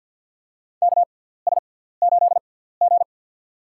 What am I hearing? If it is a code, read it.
Morse code, heard as a keyed tone, about 52 wpm, spaced as for 17 wpm, 700 Hz. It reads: KS8G
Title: USA Call Signs - 4 Characters - 6x Character Spacing - 50wpm